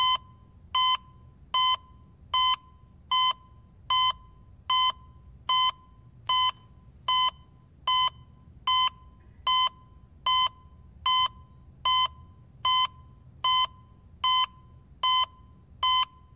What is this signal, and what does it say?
There is no code.